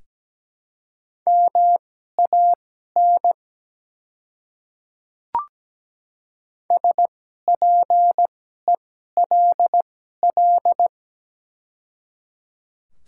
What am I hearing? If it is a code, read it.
Morse code, heard as a keyed tone, about 17 wpm, spaced as for 12 wpm, 700 Hz. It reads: MAN SPELL
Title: 2x Character Spacing - Top 300 Words - 17wpm